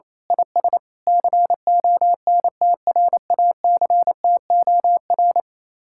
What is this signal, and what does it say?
Morse code, heard as a keyed tone, about 26 wpm, 700 Hz. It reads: IS CONTRACTOR
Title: Speed-Racing Common QSO Phrases - 28wpm